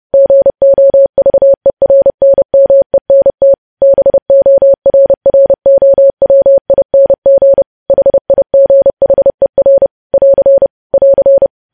Morse code, heard as a keyed tone, about 30 wpm, 550 Hz. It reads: GOVERNMENT BORROWING HIGHER <AR> <AR>
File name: podcast / News headlines-1x-30x30